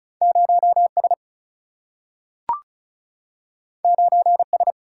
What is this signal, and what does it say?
Morse code, heard as a keyed tone, about 36 wpm, 700 Hz. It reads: 0S 9S
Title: Single-Letter Single-Number 35wpm